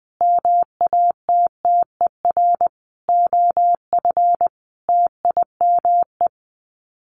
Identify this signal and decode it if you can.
Morse code, heard as a keyed tone, about 20 wpm, 700 Hz. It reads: MATTER OF TIME
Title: Sets of 5 Words Encore 20wpm